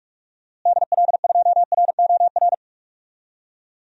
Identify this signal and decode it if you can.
Morse code, heard as a keyed tone, about 41 wpm, 700 Hz. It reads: DL2ROR
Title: DXCC-100 Call Signs - 6 Characters - Repeated 2x - 45wpm